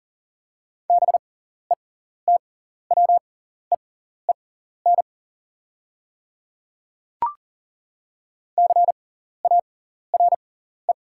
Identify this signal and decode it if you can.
Morse code, heard as a keyed tone, about 38 wpm, spaced as for 14 wpm, 690 Hz. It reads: BETWEEN CARE
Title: 6x Character Spacing - Top 300 Words - 40wpm